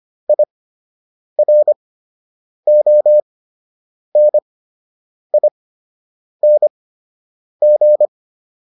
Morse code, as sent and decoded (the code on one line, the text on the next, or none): .. .-. --- -. .. -. --.
IRONING